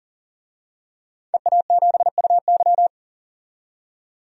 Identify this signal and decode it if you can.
Morse code, heard as a keyed tone, about 40 wpm, 700 Hz. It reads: EA7UY
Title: DXCC-100 Call Signs - 5 Characters - 40wpm